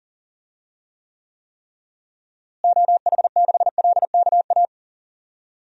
Morse code, heard as a keyed tone, about 40 wpm, 700 Hz. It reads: OH6LKA
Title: DXCC-100 Call Signs - 6 Characters - Repeated 3x - 40wpm